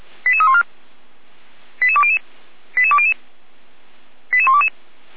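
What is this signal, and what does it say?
Five-tone selective call (ZVEI), heard as quick runs of five tones, tones of about 70 ms. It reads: 88314 88200 88200 88122
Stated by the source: Five-tone ZVEI selective call received by SDR, channel 455.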